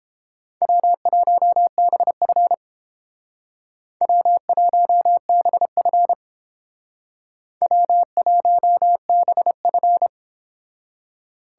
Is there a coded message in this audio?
Yes: Morse code from 0.6 to 10.1 s, heard as a keyed tone, about 28 wpm, 700 Hz. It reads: W1BF W1BF W1BF